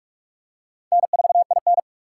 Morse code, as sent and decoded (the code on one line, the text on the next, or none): -. ....- .. -.
N4IN